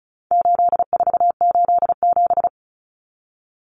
Morse code, heard as a keyed tone, about 34 wpm, 700 Hz. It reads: 8487
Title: Common Cut Numbers - 4-Digit 35wpm